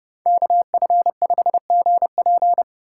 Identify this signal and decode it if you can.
Morse code, heard as a keyed tone, about 29 wpm, 700 Hz. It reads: KF5GP